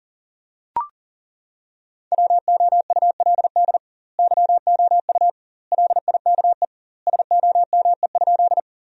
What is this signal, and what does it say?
Morse code, heard as a keyed tone, about 40 wpm, 700 Hz. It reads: WOULD YOU LIKE SOME?
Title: 2x Word-Spacing - Sentences from Top 100 Words - 40wpm